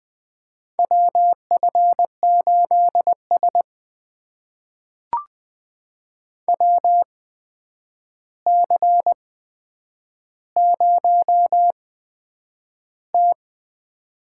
Morse code, heard as a keyed tone, about 20 wpm, spaced as for 9 wpm, 700 Hz. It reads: WF8S WC0T
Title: USA Call Signs - 4 Characters - 8x Character Spacing - 20wpm